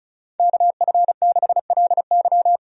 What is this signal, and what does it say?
Morse code, heard as a keyed tone, about 34 wpm, 700 Hz. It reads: KF6LY
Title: USA Call Signs - 5 Characters - 35wpm